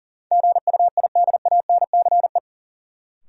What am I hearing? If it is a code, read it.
Morse code, heard as a keyed tone, about 40 wpm, 700 Hz. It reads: GUIDANCE
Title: Sentences from Top 1600 Words - New Word Drill Alt - 40wpm